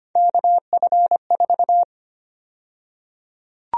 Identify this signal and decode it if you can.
Morse code, heard as a keyed tone, about 25 wpm, 700 Hz. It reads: KF4
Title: USA Call Sign Prefixes - Repeated 2x - 25wpm